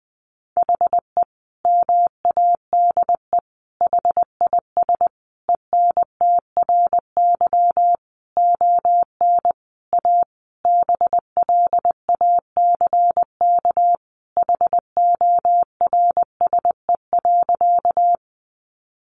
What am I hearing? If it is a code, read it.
Morse code, heard as a keyed tone, about 20 wpm, 700 Hz. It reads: HE MADE HIS ENTRY ON A BLACK HORSE.
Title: Sentences from Top 1500 Words 20wpm